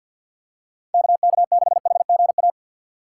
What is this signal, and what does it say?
Morse code, heard as a keyed tone, about 50 wpm, 700 Hz. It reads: KK6HGA
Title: USA Call Signs - 6 Characters - 3x Character Spacing - 50wpm